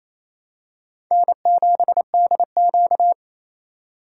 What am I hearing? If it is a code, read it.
Morse code, heard as a keyed tone, about 28 wpm, 700 Hz. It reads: N7DQ
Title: USA Call Signs - 4 Characters - 2x Character Spacing - 28wpm